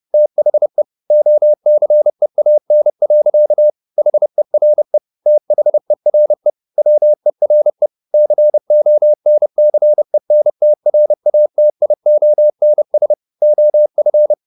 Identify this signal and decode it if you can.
Morse code, heard as a keyed tone, about 30 wpm, 600 Hz. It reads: THE OCEAN. HERE THERE WERE CONCENTRATIONS OF
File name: the old man and the sea / the old man and the sea 1 30wpm